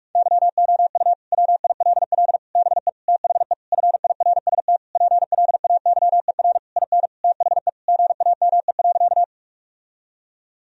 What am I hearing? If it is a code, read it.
Morse code, heard as a keyed tone, about 42 wpm, 700 Hz. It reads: YOU WILL BE THE FIRST PLAYER IN THE GAME.
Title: Sentences from Top 1000 Words - Review Part 4 - 45wpm